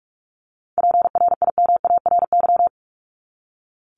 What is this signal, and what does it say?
Morse code, heard as a keyed tone, about 44 wpm, 700 Hz. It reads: PRIMARY
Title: LY Suffix - Top 100 Words with Base Words 45wpm